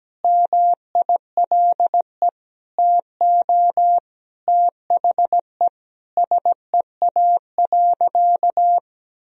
Morse code, heard as a keyed tone, about 17 wpm, 700 Hz. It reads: MILE TO THE SEA.
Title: Sentences from Top 1000 Words - Review Part 1 - 17wpm